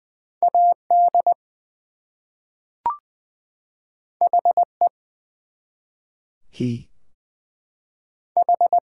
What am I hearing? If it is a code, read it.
Morse code, heard as a keyed tone, about 20 wpm, 700 Hz. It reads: AD HE H
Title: Two Letter Words - 20wpm